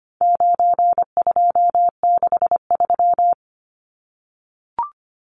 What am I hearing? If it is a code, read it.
Morse code, heard as a keyed tone, about 25 wpm, 700 Hz. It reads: 9263
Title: Numbers - 4-Digit 25wpm